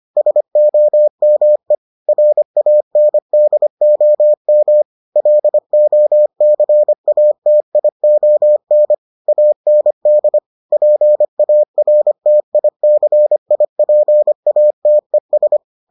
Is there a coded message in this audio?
Yes: Morse code from 0.2 to 15.6 s, heard as a keyed tone, about 25 wpm, 600 Hz. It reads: SOME RANDOM LOCATION AND PARTICIPATES